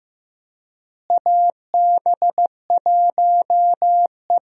Morse code, heard as a keyed tone, about 15 wpm, 700 Hz. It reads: AB1E
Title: USA Call Signs - 4 Characters - Repeated 3x - 15wpm